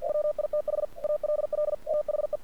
Morse code, heard as a keyed tone, about 47 wpm, 600 Hz. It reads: WITH ALL THE
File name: archived / CRSnet